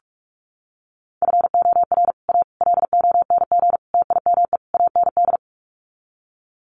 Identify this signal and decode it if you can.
Morse code, heard as a keyed tone, about 44 wpm, 700 Hz. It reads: FOR A LONG TIME AND
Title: Sets of 5 Words Encore2 45wpm